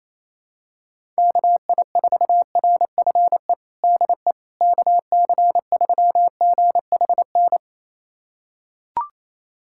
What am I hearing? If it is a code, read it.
Morse code, heard as a keyed tone, about 28 wpm, 700 Hz. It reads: KI4RFE DE KC3GHN
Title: USA Callsign DE Callsign - 6 Characters - 28wpm